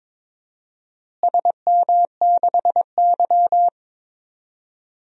Morse code, heard as a keyed tone, about 21 wpm, 700 Hz. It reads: SM6Y
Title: DXCC-100 Call Signs - 4 Characters - Repeated 3x - 22wpm